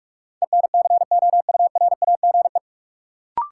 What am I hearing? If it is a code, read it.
Morse code, heard as a keyed tone, about 46 wpm, 700 Hz. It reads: ENCOURAGE
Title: Sentences from Top 1400 Words - New Word Drill - 45wpm